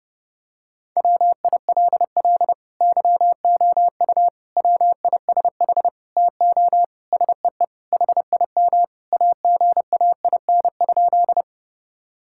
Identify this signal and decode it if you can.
Morse code, heard as a keyed tone, about 30 wpm, 700 Hz. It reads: WILL YOU WISH TO SEE HIM AGAIN?
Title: Sentences from Top 1600 Words 30wpm